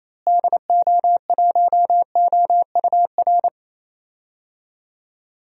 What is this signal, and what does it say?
Morse code, heard as a keyed tone, about 27 wpm, 700 Hz. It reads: DO1OUR
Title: DXCC-100 Call Signs - 6 Characters - Repeated 2x - 28wpm